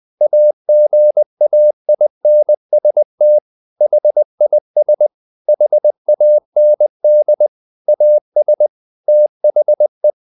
Morse code, heard as a keyed tone, about 20 wpm, 600 Hz. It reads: AGAINST HIS HAND AS THE
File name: the old man and the sea / the old man and the sea part 1 20wpm